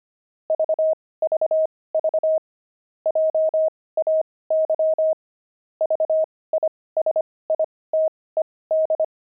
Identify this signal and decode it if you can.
Morse code, heard as a keyed tone, about 25 wpm, spaced as for 18 wpm, 630 Hz. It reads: VVV JAY VISITED
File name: pangram-fs-18-cs-25-jay